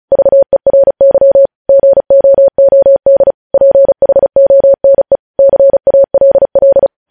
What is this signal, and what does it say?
Morse code, heard as a keyed tone, about 36 wpm, 550 Hz. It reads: VERY GOOD PHONE CALL
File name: podcast / News headlines-1x-35x35